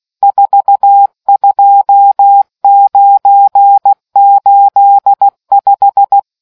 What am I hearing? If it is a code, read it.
Morse code, heard as a keyed tone, about 16 wpm, 800 Hz. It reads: 42985